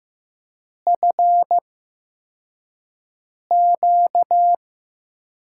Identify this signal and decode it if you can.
Morse code, heard as a keyed tone, about 15 wpm, spaced as for 4 wpm, 700 Hz. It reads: FQ
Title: USA Call Signs - 8x Character Spacing - 15wpm